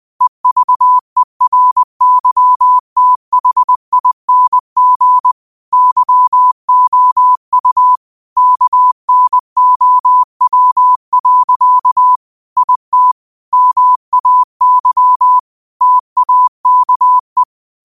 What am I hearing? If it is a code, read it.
Morse code, heard as a keyed tone, about 20 wpm, 1000 Hz. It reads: EVERYTHING YOU KNOW. IT MAY TAKE